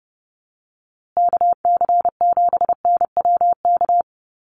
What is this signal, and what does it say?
Morse code, heard as a keyed tone, about 30 wpm, 700 Hz. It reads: KC7NWK
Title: USA Call Signs Repeated 3x 30wpm